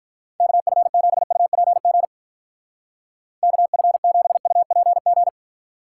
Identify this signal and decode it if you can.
Morse code, heard as a keyed tone, about 44 wpm, 700 Hz. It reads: KF7UPG KF7UPG
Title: Speed Racing - USA 2x3 Call Signs 35wpm